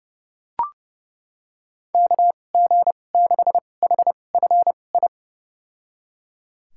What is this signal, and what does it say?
Morse code, heard as a keyed tone, about 32 wpm, spaced as for 22 wpm, 700 Hz. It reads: KG6HFI
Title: USA Call Signs - 6 Characters - 2x Character Spacing - 30wpm